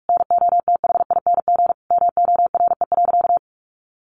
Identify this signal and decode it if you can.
Morse code, heard as a keyed tone, about 44 wpm, 700 Hz. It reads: NOTHING MORE.
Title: Sentences from Top 800 Words 45wpm